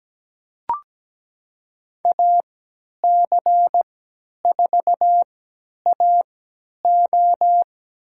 Morse code, heard as a keyed tone, about 17 wpm, spaced as for 10 wpm, 700 Hz. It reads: AC4AO
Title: USA Call Signs - 5 Characters - 3x Character Spacing - 17wpm